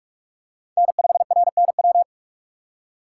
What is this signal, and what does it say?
Morse code, heard as a keyed tone, about 38 wpm, 700 Hz. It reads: N5RNW